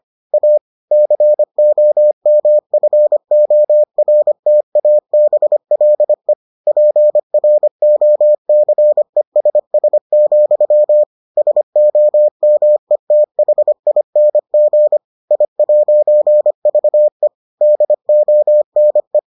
Morse code, heard as a keyed tone, about 25 wpm, 600 Hz. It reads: A COMFORTABLE PROCESS, SOMETHING I'VE DONE